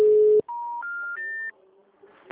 Special information tone.
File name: out- -2009- - - .595